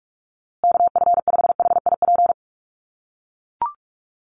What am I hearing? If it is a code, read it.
Morse code, heard as a keyed tone, about 44 wpm, 700 Hz. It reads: KF5HIP
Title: USA Call Signs Repeated 3x 45wpm